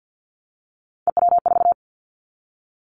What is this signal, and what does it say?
Morse code, heard as a keyed tone, about 50 wpm, 700 Hz. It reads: EW4
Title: Callsign Prefixes for Top 50 DXCC - 50wpm